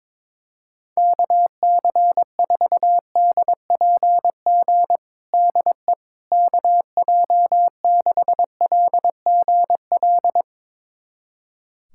Morse code, heard as a keyed tone, about 21 wpm, 700 Hz. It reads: KC4DPG DE KJ6LGL
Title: USA Callsign DE Callsign - 6 Characters - 22wpm